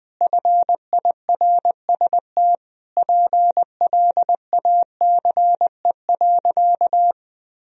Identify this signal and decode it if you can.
Morse code, heard as a keyed tone, about 20 wpm, 700 Hz. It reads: FIRST PLACE.
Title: Sentences from Top 1000 Words - Review Part 1 - 20wpm